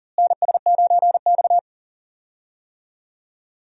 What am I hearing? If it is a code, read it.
Morse code, heard as a keyed tone, about 40 wpm, 700 Hz. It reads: NS9X
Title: USA Call Signs - 4 Characters - Repeated 2x - 40wpm